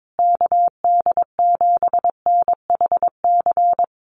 Morse code, heard as a keyed tone, about 22 wpm, 700 Hz. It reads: KD7NHC
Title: USA Call Signs Repeated 3x 22wpm